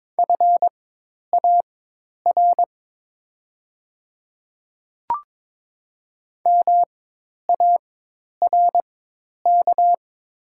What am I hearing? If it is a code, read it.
Morse code, heard as a keyed tone, about 22 wpm, spaced as for 10 wpm, 700 Hz. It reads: FAR MARK
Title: 4x Character Spacing - Top 300 Words - 22wpm